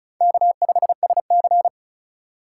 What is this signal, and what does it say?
Morse code, heard as a keyed tone, about 34 wpm, 700 Hz. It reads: K5SC
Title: USA Call Signs - 4 Characters - 2x Character Spacing - 35wpm